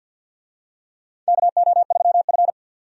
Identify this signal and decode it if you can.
Morse code, heard as a keyed tone, about 50 wpm, 700 Hz. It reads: KO3F
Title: USA Call Signs - 4 Characters - 50wpm